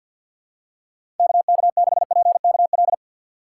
Morse code, heard as a keyed tone, about 48 wpm, 700 Hz. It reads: KK6PKL
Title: USA Call Signs - 6 Characters - 4x Character Spacing - 50wpm